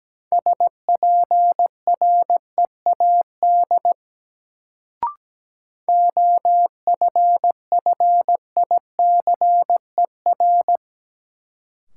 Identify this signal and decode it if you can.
Morse code, heard as a keyed tone, about 17 wpm, 700 Hz. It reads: SPREAD OFFICER
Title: Sentences from Top 1200 Words - New Word Drill Alt - 17wpm